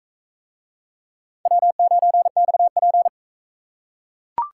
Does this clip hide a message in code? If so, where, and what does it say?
Yes: Morse code from 1.5 to 3.1 s, heard as a keyed tone, about 39 wpm, 700 Hz. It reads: W9XP